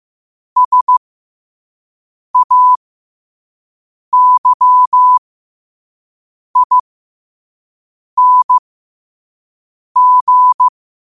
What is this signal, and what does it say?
Morse code, heard as a keyed tone, about 15 wpm, spaced as for 5 wpm, 1000 Hz. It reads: SAYING
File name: qotdfiles / 5wpm